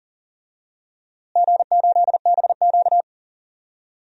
Morse code, heard as a keyed tone, about 38 wpm, 700 Hz. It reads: G8BQ
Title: DXCC-100 Call Signs - 4 Characters - Repeated 2x - 40wpm